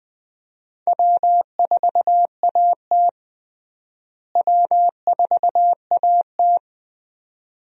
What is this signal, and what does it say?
Morse code, heard as a keyed tone, about 21 wpm, 700 Hz. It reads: W4AT W4AT
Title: USA Call Signs - 4 Characters - Repeated 2x - 20wpm